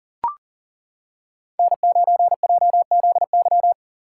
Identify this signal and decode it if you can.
Morse code, heard as a keyed tone, about 37 wpm, 700 Hz. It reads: N9JZY
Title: K1USN SST QSO Practice - 40wpm